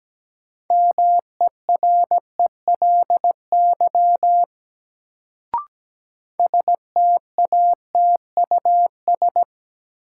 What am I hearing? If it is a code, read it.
Morse code, heard as a keyed tone, about 17 wpm, 700 Hz. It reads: MERELY STATUS
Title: Sentences from Top 1400 Words - New Word Drill Alt - 17wpm